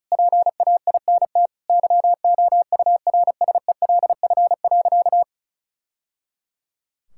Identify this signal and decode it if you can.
Morse code, heard as a keyed tone, about 34 wpm, 700 Hz. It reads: PAINT YOURSELF.
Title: Sentences from Top 1000 Words - Review Part 6 - 35wpm